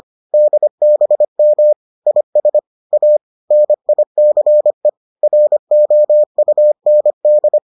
Morse code, heard as a keyed tone, about 25 wpm, 600 Hz. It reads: DBM IS A NICE ROUND